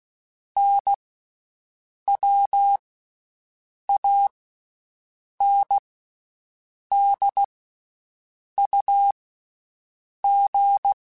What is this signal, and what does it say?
Morse code, heard as a keyed tone, about 16 wpm, 800 Hz. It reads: N W A N D U G
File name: Basic Sets / Basic Letter Groups Collective7 / AWUNDG@13